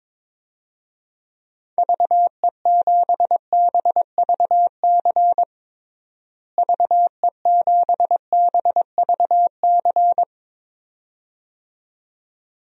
Morse code, heard as a keyed tone, about 22 wpm, 700 Hz. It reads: VE7BVC VE7BVC
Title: DXCC-100 Call Signs - 6 Characters - Repeated 2x - 22wpm